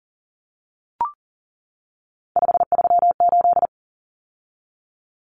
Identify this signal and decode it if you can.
Morse code, heard as a keyed tone, about 40 wpm, 700 Hz. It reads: 538
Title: Numbers - 3-Digit 40wpm